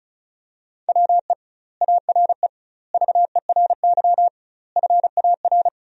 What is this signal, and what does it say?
Morse code, heard as a keyed tone, about 36 wpm, 700 Hz. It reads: WE ARE VERY FAR
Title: 2x Word-Spacing - Sentences from Top 300 Words - 35wpm